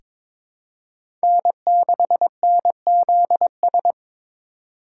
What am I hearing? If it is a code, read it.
Morse code, heard as a keyed tone, about 22 wpm, 700 Hz. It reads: N6NZS
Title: USA Call Signs - 6x Character Spacing - 22wpm